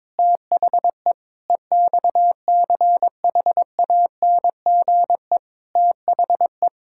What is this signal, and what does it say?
Morse code, heard as a keyed tone, about 21 wpm, 700 Hz. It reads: THE EXCHANGE THE
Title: Sentences from Top 1000 Words - Review Part 6 - 22wpm